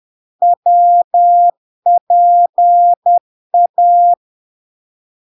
Morse code, heard as a keyed tone, about 10 wpm, 700 Hz. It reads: WPA
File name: LARK-FD-2022-Log-in-CW-10wpm-700-Hz